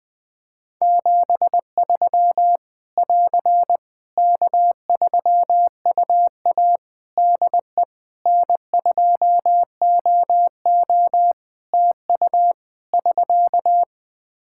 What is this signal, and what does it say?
Morse code, heard as a keyed tone, about 20 wpm, 700 Hz. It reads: 73 <AR> K3UA DE N2OO TU <SK>
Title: K1USN SST QSO Practice 20wpm